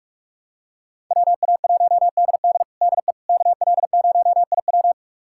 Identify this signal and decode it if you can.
Morse code, heard as a keyed tone, about 46 wpm, 700 Hz. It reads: WA1DD DE KL0IW
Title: USA Callsign DE Callsign - 5 Characters - 45wpm